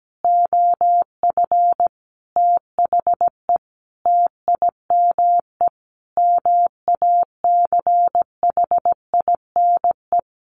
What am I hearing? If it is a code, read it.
Morse code, heard as a keyed tone, about 17 wpm, 700 Hz. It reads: OF THE TIME MACHINE
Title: A Sound Of Thunder 17wpm